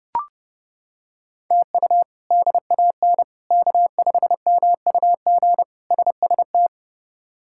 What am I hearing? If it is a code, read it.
Morse code, heard as a keyed tone, about 30 wpm, 700 Hz. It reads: TU DAN K5MUG SST
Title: K1USN SST QSO Practice - 30wpm